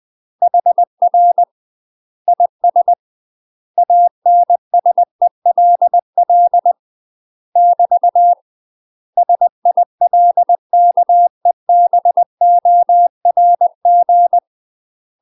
Morse code, heard as a keyed tone, about 20 wpm, 700 Hz. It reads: HR IS ANSELL <BT> SILKEBORG
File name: UK MASTER QSOs TRAINING / 20WPM / UK MASTER QSOs 0057